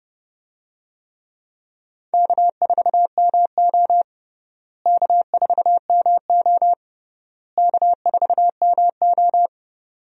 Morse code, heard as a keyed tone, about 30 wpm, 700 Hz. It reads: K4MO K4MO K4MO
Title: USA Call Signs - 4 Characters - Repeated 3x - 30wpm